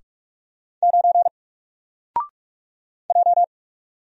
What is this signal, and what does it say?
Morse code, heard as a keyed tone, about 45 wpm, 700 Hz. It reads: 9 J
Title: Single Character (Letter:Number:Punctuation:Prosign) Alt 45wpm